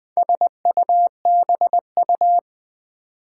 Morse code, heard as a keyed tone, about 20 wpm, 700 Hz. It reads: SUBU
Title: Build A Word - 2 Letters At A Time - 8-Letter Words - 20wpm